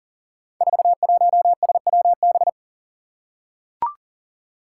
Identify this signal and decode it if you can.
Morse code, heard as a keyed tone, about 43 wpm, 700 Hz. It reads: 41SWB
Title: Five Challenging Letter-Number Combinations 40wpm